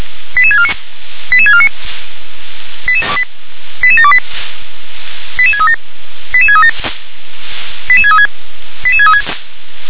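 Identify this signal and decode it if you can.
Five-tone selective call (ZVEI), heard as quick runs of five tones, tones of about 70 ms. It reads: 88649 88649 88728 88637 88637 88636 88636
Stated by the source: ZVEI five-tone selective call received by SDR.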